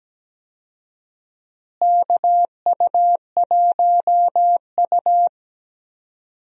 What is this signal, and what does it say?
Morse code, heard as a keyed tone, about 17 wpm, 700 Hz. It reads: KU1U
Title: USA Call Signs - 4 Characters - Repeated 2x - 17wpm